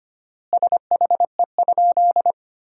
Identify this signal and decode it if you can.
Morse code, heard as a keyed tone, about 25 wpm, 700 Hz. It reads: SHE?
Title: 4x Word-Spacing - Sentences from Top 100 Words - 25wpm